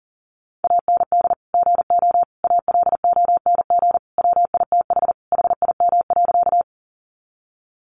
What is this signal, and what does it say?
Morse code, heard as a keyed tone, about 40 wpm, 700 Hz. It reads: AND GO ALONG WITH HIM.
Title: Sentences from Top 700 Words 40wpm